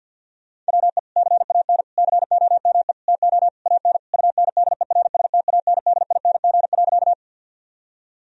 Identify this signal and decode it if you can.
Morse code, heard as a keyed tone, about 50 wpm, 700 Hz. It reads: WE CAN COME TO AN UNDERSTANDING.
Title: Sentences from Top 1000 Words - Review Part 3 - 50wpm